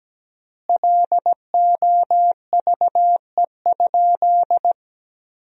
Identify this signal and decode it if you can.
Morse code, heard as a keyed tone, about 17 wpm, 700 Hz. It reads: LOVE?
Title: 4x Word-Spacing - Sentences from Top 400 Words - 17wpm